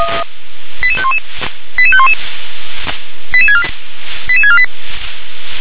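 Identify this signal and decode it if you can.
Five-tone selective call (ZVEI), heard as quick runs of five tones, tones of about 70 ms. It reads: 88511 88511 88648
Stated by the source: Received on an SDR; ZVEI five-tone selective call.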